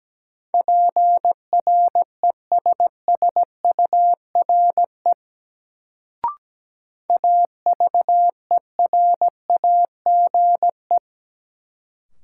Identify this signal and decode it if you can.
Morse code, heard as a keyed tone, about 17 wpm, 700 Hz. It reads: PRESSURE AVERAGE
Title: Sentences from Top 1000 Words - Review Part 6 - New Word Drill - 17wpm